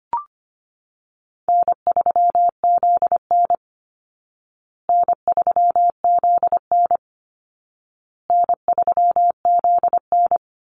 Morse code, heard as a keyed tone, about 25 wpm, 700 Hz. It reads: N3ZN N3ZN N3ZN
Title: Call Signs Repeated 2x - 25wpm